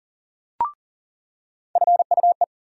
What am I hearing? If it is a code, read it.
Morse code, heard as a keyed tone, about 40 wpm, 700 Hz. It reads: FUE
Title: USA Call Sign Postfixes - 3 Letters 40wpm